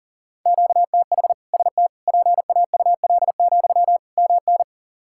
Morse code, heard as a keyed tone, about 38 wpm, 700 Hz. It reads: QTH ST PAUL, MN
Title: K1USN SST QSO Practice 40wpm